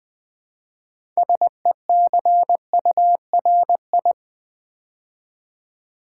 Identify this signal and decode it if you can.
Morse code, heard as a keyed tone, about 20 wpm, 700 Hz. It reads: SECURI
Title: Build A Word - 2 Letters At A Time - 8-Letter Words - 20wpm